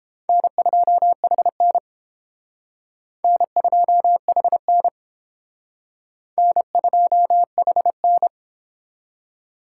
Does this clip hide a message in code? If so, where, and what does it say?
Yes: Morse code from 0.3 to 8.3 s, heard as a keyed tone, about 29 wpm, 700 Hz. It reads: N2HN N2HN N2HN